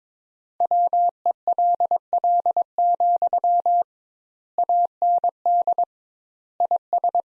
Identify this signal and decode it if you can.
Morse code, heard as a keyed tone, about 22 wpm, 700 Hz. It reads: WELL, AND IS
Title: Sentences Top 200 Words - 22wpm